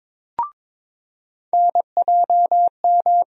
Morse code, heard as a keyed tone, about 22 wpm, 700 Hz. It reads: NJM
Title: USA Call Sign Postfixes - 3 Letters 22wpm